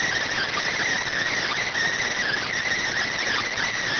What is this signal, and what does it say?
Emergency Alert System (SAME) tones, heard as harsh buzzing bursts. It is heard from 0.3 to 3.8 s.